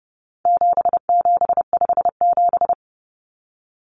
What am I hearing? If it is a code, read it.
Morse code, heard as a keyed tone, about 30 wpm, 700 Hz. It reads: 7757
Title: Numbers - 4-Digit 30wpm